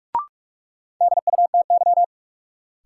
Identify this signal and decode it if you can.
Morse code, heard as a keyed tone, about 46 wpm, 700 Hz. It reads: DUTY
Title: Sentences from Top 1500 Words - New Word Drill - 45wpm